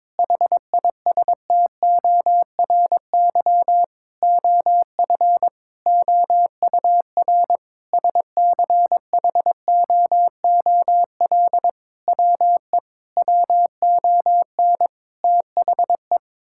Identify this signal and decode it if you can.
Morse code, heard as a keyed tone, about 22 wpm, 700 Hz. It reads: HISTORY OF OUR SCHOOL WE WON THE
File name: practice / 22wpm / Sentences with 8-Word Phrases - 22wpm